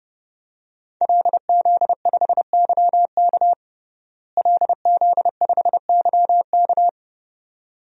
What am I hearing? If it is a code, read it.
Morse code, heard as a keyed tone, about 30 wpm, 700 Hz. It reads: LZ5YK LZ5YK
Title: DXCC-100 Call Signs - 5 Characters - Repeated 2x - 30wpm